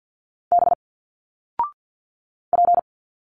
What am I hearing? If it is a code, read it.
Morse code, heard as a keyed tone, about 50 wpm, 700 Hz. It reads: B P
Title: Single Letters 50wpm